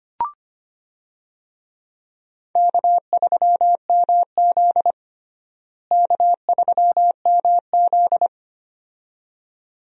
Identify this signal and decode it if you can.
Morse code, heard as a keyed tone, about 25 wpm, 700 Hz. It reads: K3MZ K3MZ
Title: USA Call Signs - 4 Characters - Repeated 2x - 25wpm